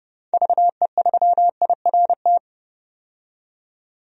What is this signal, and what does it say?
Morse code, heard as a keyed tone, about 29 wpm, 700 Hz. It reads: VE3IRT
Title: DXCC-100 Call Signs - 6 Characters - 30wpm